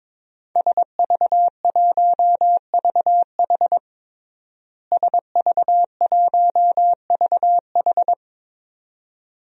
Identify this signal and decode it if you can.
Morse code, heard as a keyed tone, about 21 wpm, 700 Hz. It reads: SV1VH SV1VH
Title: DXCC-100 Call Signs - 5 Characters - Repeated 2x - 22wpm